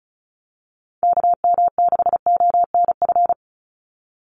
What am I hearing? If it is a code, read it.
Morse code, heard as a keyed tone, about 34 wpm, 700 Hz. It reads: KM6ONF
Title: USA Call Signs 35wpm